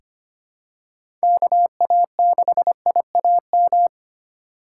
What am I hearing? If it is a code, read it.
Morse code, heard as a keyed tone, about 25 wpm, 700 Hz. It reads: KA6IAM
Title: USA Call Signs - 2x Character Spacing - 25wpm